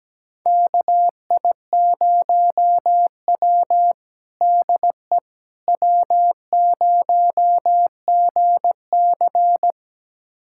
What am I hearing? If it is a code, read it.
Morse code, heard as a keyed tone, about 17 wpm, 700 Hz. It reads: KI0W DE W0GC
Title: USA Callsign DE Callsign - 4 Characters - 17wpm